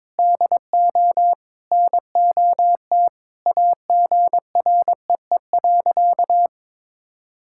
Morse code, heard as a keyed tone, about 22 wpm, 700 Hz. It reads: DO NOT AGREE.
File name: practice / 22wpm / Sentences with 4-Word Phrases - 22wpm